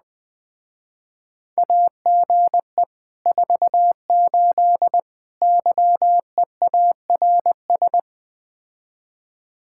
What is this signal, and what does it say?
Morse code, heard as a keyed tone, about 20 wpm, 700 Hz. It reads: AGE 48 YEARS
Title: Speed-Racing Common QSO Phrases - 17wpm